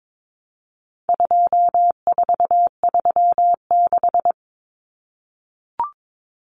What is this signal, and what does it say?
Morse code, heard as a keyed tone, about 22 wpm, 700 Hz. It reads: 2436
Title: Numbers - 4-Digit 22wpm